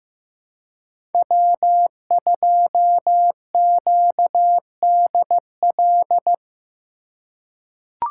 Morse code, heard as a keyed tone, about 15 wpm, 700 Hz. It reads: W2QDL
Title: USA Call Signs - 5 Characters - 3x Character Spacing - 15wpm